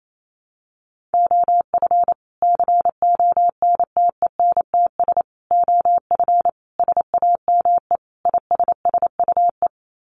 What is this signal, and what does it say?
Morse code, heard as a keyed tone, about 28 wpm, 700 Hz. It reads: OF CONTENTS OF SAME ISSUE
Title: Sets of 5 Words Encore2 28wpm